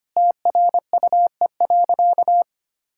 Morse code, heard as a keyed tone, about 25 wpm, 700 Hz. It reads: TRUE.
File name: practice / 25wpm / Sentences with 4-Word Phrases - 25wpm